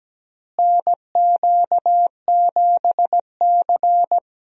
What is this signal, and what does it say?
Morse code, heard as a keyed tone, about 17 wpm, 700 Hz. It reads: NQ7C
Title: USA Call Signs - 4 Characters - Repeated 3x - 17wpm